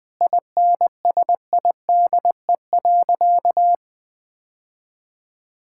Morse code, heard as a keyed tone, about 20 wpm, 700 Hz. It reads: INSIDE.
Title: Sentences from Top 1000 Words - Review Part 6 - 20wpm